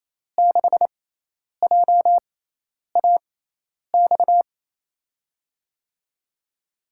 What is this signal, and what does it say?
Morse code, heard as a keyed tone, about 27 wpm, spaced as for 10 wpm, 700 Hz. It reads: 6JAX
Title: USA Call Signs - 6x Character Spacing - 28wpm